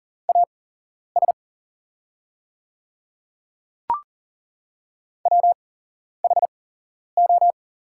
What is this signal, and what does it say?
Morse code, heard as a keyed tone, about 40 wpm, spaced as for 11 wpm, 700 Hz. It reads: AS WHO
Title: 8x Character Spacing - Top 100 Words - 40wpm